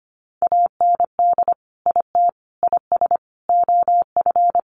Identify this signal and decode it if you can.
Morse code, heard as a keyed tone, about 25 wpm, 700 Hz. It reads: AND IT IS OF
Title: Sentences from Top 1100 Words 25wpm